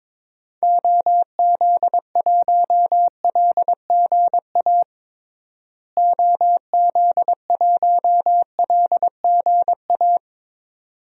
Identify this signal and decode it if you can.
Morse code, heard as a keyed tone, about 21 wpm, 700 Hz. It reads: OZ1LGA OZ1LGA
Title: DXCC-100 Call Signs - 6 Characters - Repeated 2x - 22wpm